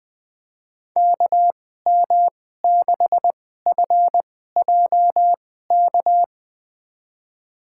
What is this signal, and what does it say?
Morse code, heard as a keyed tone, about 21 wpm, spaced as for 15 wpm, 700 Hz. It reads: KM6FJK
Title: USA Call Signs - 2x Character Spacing - 20wpm